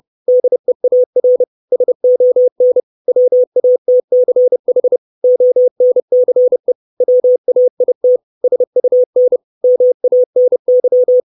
Morse code, heard as a keyed tone, about 30 wpm, 500 Hz. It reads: DEAR SON WATCH ONCE WAIT SUN MANY
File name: 632 Words 30WPM